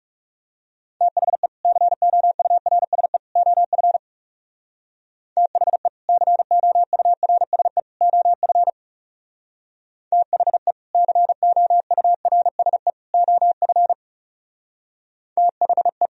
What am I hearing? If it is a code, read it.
Morse code, heard as a keyed tone, about 38 wpm, 700 Hz. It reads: THE COURSE OF THE COURSE OF THE COURSE OF THE